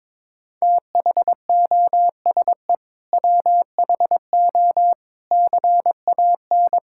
Morse code, heard as a keyed tone, about 21 wpm, 700 Hz. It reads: THOSE WHO CAN
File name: practice / 22wpm / Sentences with 8-Word Phrases - 22wpm